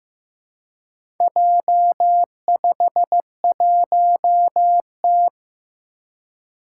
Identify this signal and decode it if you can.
Morse code, heard as a keyed tone, about 15 wpm, 700 Hz. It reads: J51T
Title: Four Challenging Letter-Number Combinations 15wpm